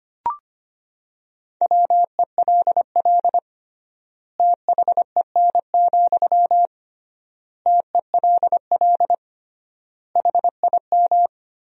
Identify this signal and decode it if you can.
Morse code, heard as a keyed tone, about 25 wpm, 700 Hz. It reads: WELL THEN, TELL HIM
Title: 3x Word-Spacing - Sentences from Top 200 Words - 25wpm